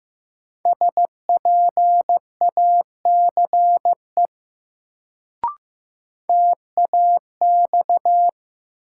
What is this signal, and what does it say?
Morse code, heard as a keyed tone, about 15 wpm, 700 Hz. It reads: SPACE TAX
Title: Sentences from Top 1000 Words - Review Part 5 - New Word Drill - 15wpm